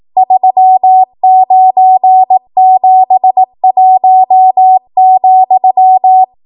Morse code, heard as a keyed tone, about 18 wpm, 750 Hz. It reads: 3971,
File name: Morse / 18WPM